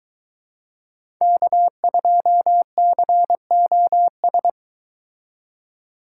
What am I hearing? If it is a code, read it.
Morse code, heard as a keyed tone, about 23 wpm, 700 Hz. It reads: K2COS